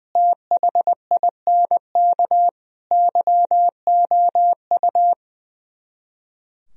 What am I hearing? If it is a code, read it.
Morse code, heard as a keyed tone, about 20 wpm, 700 Hz. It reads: THINK YOU
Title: Speed-Racing - Sets of 5 Words 20wpm